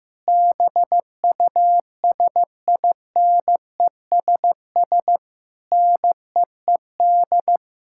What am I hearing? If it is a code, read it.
Morse code, heard as a keyed tone, about 15 wpm, 700 Hz. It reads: BUSINESS NEED
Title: Sentences from Top 1000 Words - Review Part 6 - 15wpm